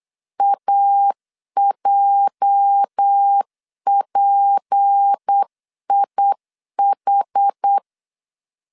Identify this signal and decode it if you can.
Morse code, heard as a keyed tone, about 8 wpm, 800 Hz. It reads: AJPIH
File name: LEZIONI / 9 286 8wpm 3425